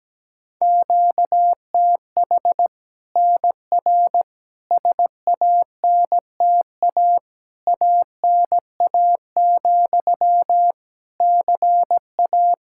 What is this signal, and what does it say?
Morse code, heard as a keyed tone, about 17 wpm, 700 Hz. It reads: QTH NR SANTA ANA, CA
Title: Practice QSO - Exchange 1 - Beginning - 17wpm